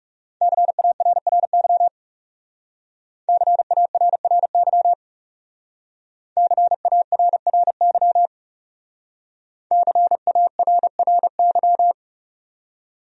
Morse code, heard as a keyed tone, about 40 wpm, 700 Hz. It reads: CARRY CARRY CARRY CARRY